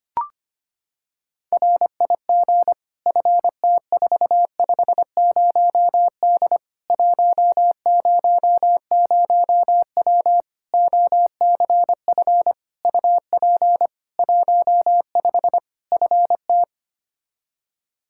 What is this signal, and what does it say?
Morse code, heard as a keyed tone, about 25 wpm, 700 Hz. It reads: RIG FT450D 100W OCF UP 15 FT
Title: Practice QSO - Exchange 2 - Info - 25wpm